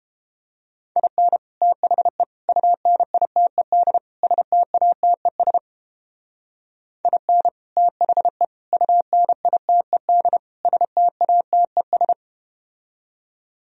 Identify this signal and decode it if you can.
Morse code, heard as a keyed tone, about 32 wpm, 700 Hz. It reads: IN THE UNITED STATES IN THE UNITED STATES